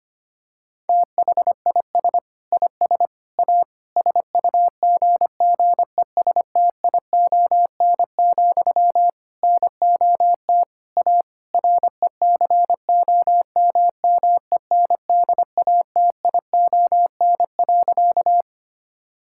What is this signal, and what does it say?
Morse code, heard as a keyed tone, about 25 wpm, 700 Hz. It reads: THIS IS A SUGGESTION, NOT A RECOMMENDATION.
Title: Sentences from Top 1600 Words 25wpm